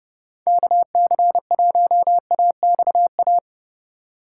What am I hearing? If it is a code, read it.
Morse code, heard as a keyed tone, about 36 wpm, 700 Hz. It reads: KC1AXA
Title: Speed Racing - USA 2x3 Call Signs 30wpm